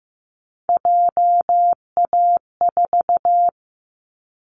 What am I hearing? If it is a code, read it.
Morse code, heard as a keyed tone, about 15 wpm, 700 Hz. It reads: JA4
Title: Callsign Prefixes for Top 25 DXCC - 15wpm